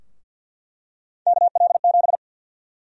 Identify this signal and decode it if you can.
Morse code, heard as a keyed tone, about 45 wpm, 700 Hz. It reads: KL7
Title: USA Call Sign Prefixes - 50wpm